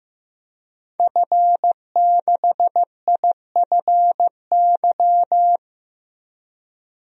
Morse code, heard as a keyed tone, about 15 wpm, 700 Hz. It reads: F6IFY
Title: DXCC-100 Call Signs - 5 Characters - 15wpm